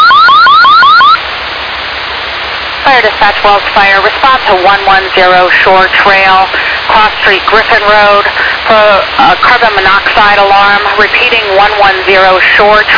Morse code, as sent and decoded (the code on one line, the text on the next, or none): none